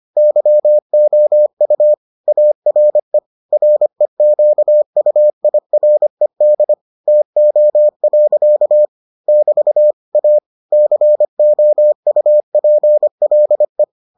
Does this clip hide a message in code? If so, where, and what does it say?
Yes: Morse code from 0.2 to 13.8 s, heard as a keyed tone, about 25 wpm, 600 Hz. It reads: YOU ARE REQUIRED TO. <BT> A COUPLE